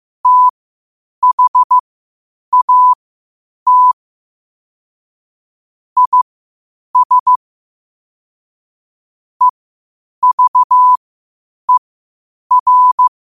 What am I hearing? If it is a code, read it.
Morse code, heard as a keyed tone, about 15 wpm, spaced as for 8 wpm, 1000 Hz. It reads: THAT IS EVER
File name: qotdfiles / 8wpm